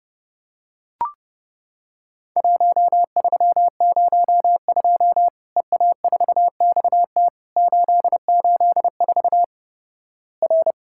Morse code, heard as a keyed tone, about 32 wpm, 700 Hz. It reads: 1302 EA4XT 884 R
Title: Worked All Europe DX Contest - QTC Exchange - 30wpm